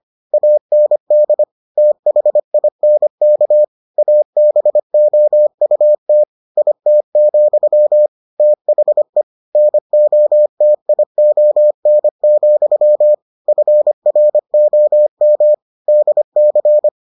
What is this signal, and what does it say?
Morse code, heard as a keyed tone, about 25 wpm, 600 Hz. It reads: AND THINK ABOUT IT, THE NOTION, FROM DC